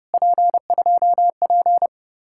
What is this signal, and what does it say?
Morse code, heard as a keyed tone, about 34 wpm, 700 Hz. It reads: P2P